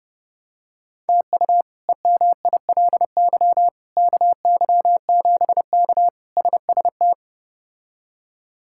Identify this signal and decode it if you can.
Morse code, heard as a keyed tone, about 30 wpm, 700 Hz. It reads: TU EMILY KY7K SST
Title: K1USN SST QSO Practice - 30wpm